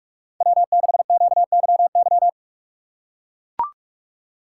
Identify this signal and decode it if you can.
Morse code, heard as a keyed tone, about 44 wpm, 700 Hz. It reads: W6QYY
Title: USA Call Signs - 5 Characters - 6x Character Spacing - 45wpm